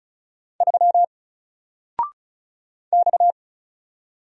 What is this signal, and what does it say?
Morse code, heard as a keyed tone, about 35 wpm, 700 Hz. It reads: 3 X
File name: Single-Letter-Number-35wpm